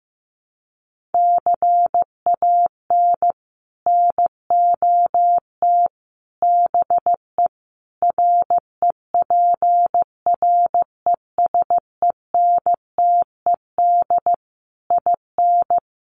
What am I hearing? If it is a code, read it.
Morse code, heard as a keyed tone, about 15 wpm, 700 Hz. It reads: CAN NOT BE REPRESENTED IN
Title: Sets of 5 Words 15wpm